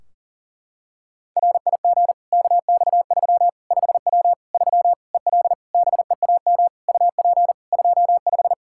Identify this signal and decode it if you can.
Morse code, heard as a keyed tone, about 40 wpm, 700 Hz. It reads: RIG KX3 5W 3 EL BEAM UP 25
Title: K1USN SST QSO Practice 40wpm